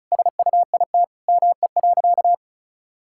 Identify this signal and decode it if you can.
Morse code, heard as a keyed tone, about 34 wpm, 700 Hz. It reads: SUIT ME.
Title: Sentences from Top 1600 Words 35wpm